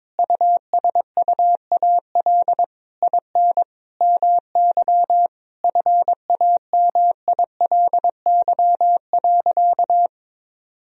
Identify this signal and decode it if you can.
Morse code, heard as a keyed tone, about 21 wpm, 700 Hz. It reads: USUAL IN MY FAMILY.
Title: Sentences from Top 1000 Words - Review Part 2 - 22wpm